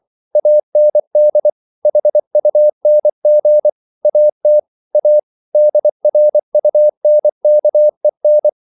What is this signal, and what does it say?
Morse code, heard as a keyed tone, about 24 wpm, 600 Hz. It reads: AND HUNG AT A DRUNKEN